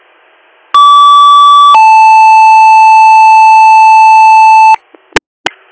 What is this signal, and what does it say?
Two-tone page, heard as a one-second tone, then a three-second tone, about 1150 then 880 Hz, with noise. It is heard from 0.7 to 4.8 s.